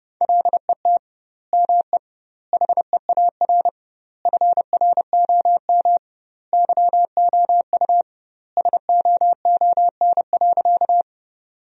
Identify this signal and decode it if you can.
Morse code, heard as a keyed tone, about 30 wpm, 700 Hz. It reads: LET ME HEAR FROM YOU SOON.
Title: 2x Word-Spacing - Sentences from Top 400 Words - 30wpm